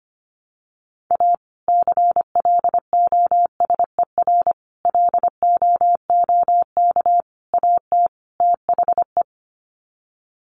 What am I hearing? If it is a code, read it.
Morse code, heard as a keyed tone, about 25 wpm, 700 Hz. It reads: A CLOSER LOOK AT THE
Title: Sets of 5 Words Encore 25wpm